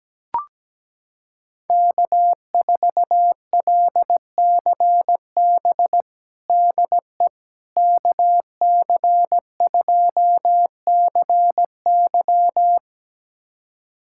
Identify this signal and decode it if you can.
Morse code, heard as a keyed tone, about 17 wpm, 700 Hz. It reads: K4LCB DE KC2CY
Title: USA Callsign DE Callsign - 5 Characters - 17wpm